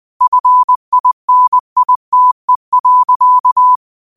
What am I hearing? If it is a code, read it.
Morse code, heard as a keyed tone, about 20 wpm, 1000 Hz. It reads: FINITE.
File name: qotdfiles / 20wpm